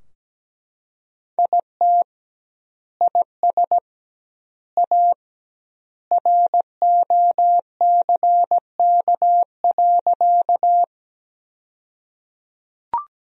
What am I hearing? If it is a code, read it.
Morse code, heard as a keyed tone, about 17 wpm, 700 Hz. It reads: IT IS A ROCK.
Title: 2x Word-Spacing - Sentences from Top 400 Words - 17wpm